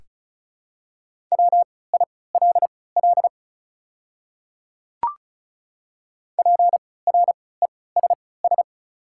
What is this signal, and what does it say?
Morse code, heard as a keyed tone, about 36 wpm, spaced as for 20 wpm, 690 Hz. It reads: WILL PRESS
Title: 3x Character Spacing - Top 500 Words - 35wpm